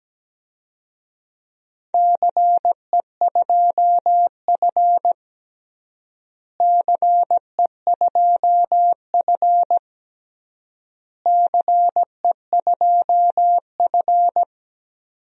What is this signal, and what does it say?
Morse code, heard as a keyed tone, about 17 wpm, 700 Hz. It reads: CE2F CE2F CE2F
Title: DXCC-100 Call Signs - 4 Characters - Repeated 3x - 17wpm